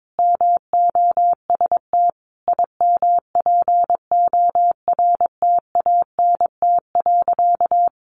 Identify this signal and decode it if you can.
Morse code, heard as a keyed tone, about 22 wpm, 700 Hz. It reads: MOST IMPORTANT.
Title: Sentences from Top 1200 Words 22wpm